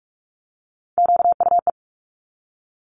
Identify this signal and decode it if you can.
Morse code, heard as a keyed tone, about 46 wpm, 700 Hz. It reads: QUE
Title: Multi-letter Phonemes 45wpm